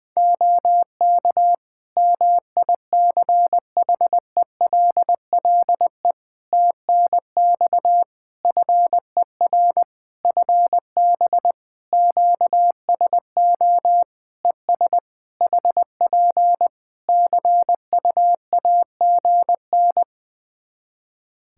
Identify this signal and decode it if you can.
Morse code, heard as a keyed tone, about 20 wpm, 700 Hz. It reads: OK MICHELLE TNX FER FB QSO ES HP CUAGN
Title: Practice QSO - Exchange 4 - Ending - 20wpm